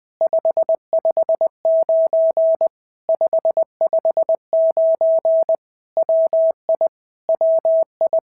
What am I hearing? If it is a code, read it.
Morse code, heard as a keyed tone, about 20 wpm, 650 Hz. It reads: 559 559 WI WI
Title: POTA QSO Practice - 20wpm